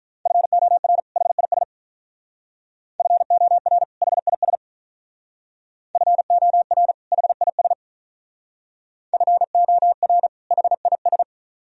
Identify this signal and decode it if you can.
Morse code, heard as a keyed tone, about 43 wpm, 700 Hz. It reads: FOR HIS FOR HIS FOR HIS FOR HIS
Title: Speed-Racing - Sets of 2 Words 35wpm